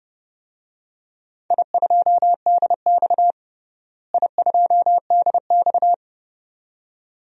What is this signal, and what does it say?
Morse code, heard as a keyed tone, about 31 wpm, 700 Hz. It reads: I2DX I2DX
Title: DXCC-100 Call Signs - 4 Characters - Repeated 2x - 30wpm